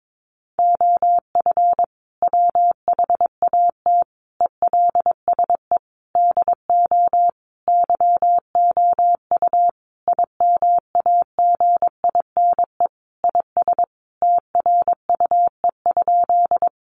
Morse code, heard as a keyed tone, about 22 wpm, 700 Hz. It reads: OF WHAT ELSE DO YOU IMAGINE IS TRUE?
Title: Sentences from Top 1400 Words 22wpm